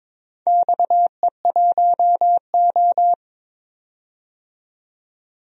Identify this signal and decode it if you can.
Morse code, heard as a keyed tone, about 22 wpm, 700 Hz. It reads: XE1O
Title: DXCC-100 Call Signs - 4 Characters - 22wpm